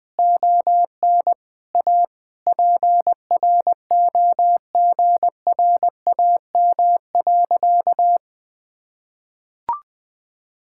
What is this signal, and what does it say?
Morse code, heard as a keyed tone, about 21 wpm, 700 Hz. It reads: ON A PROGRAM.
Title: Sentences from Top 1000 Words - Review Part 5 - 20wpm